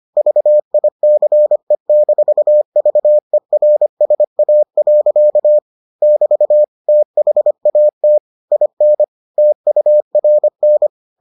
Morse code, heard as a keyed tone, about 25 wpm, 600 Hz. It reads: VICE-VERSA. <BT> THAT IN TURN